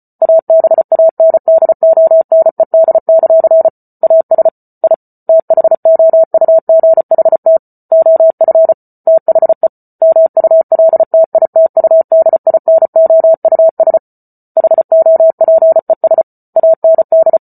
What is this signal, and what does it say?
Morse code, heard as a keyed tone, about 34 wpm, 650 Hz. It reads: ABANDONED; AS I THOUGHT OF THE MULTITUDINOUS HOPES AND